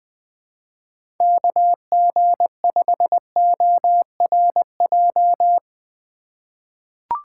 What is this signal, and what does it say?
Morse code, heard as a keyed tone, about 20 wpm, 700 Hz. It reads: KG5ORJ
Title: USA Call Signs - 6 Characters - 3x Character Spacing - 20wpm